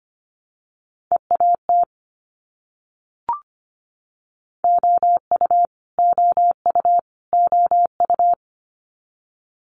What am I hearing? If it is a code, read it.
Morse code, heard as a keyed tone, about 25 wpm, 700 Hz. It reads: EAT OU OU OU
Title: Multi-letter Phonemes Repeated 3x 25wpm